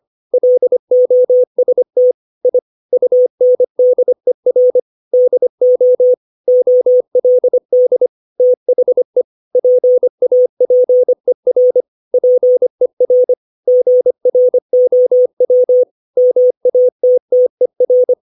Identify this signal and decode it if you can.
Morse code, heard as a keyed tone, about 25 wpm, 500 Hz. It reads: LOST I UNDER DO OLD THE PAPER PER GROW MATTER
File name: 632 words / 632 Words 25WPM